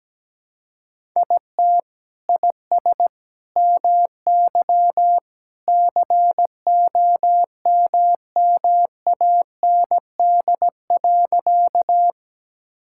Morse code, heard as a keyed tone, about 17 wpm, 700 Hz. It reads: IT IS MY COMMAND.
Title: Sentences from Top 1600 Words 17wpm